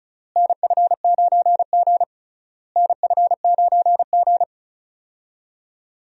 Morse code, heard as a keyed tone, about 36 wpm, 700 Hz. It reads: NF9G NF9G
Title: USA Call Signs - 4 Characters - Repeated 3x - 35wpm